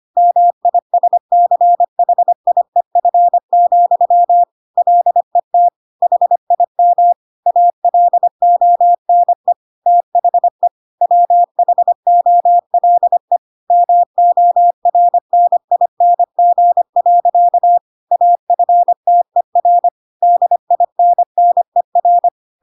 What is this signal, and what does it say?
Morse code, heard as a keyed tone, about 25 wpm, 700 Hz. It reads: MISCHIEF, LET HIM ALONE THE WHOLE MORNING. AFTER DINNER
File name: Pinocchio / Pinocchio ch8 25 25 ik1hge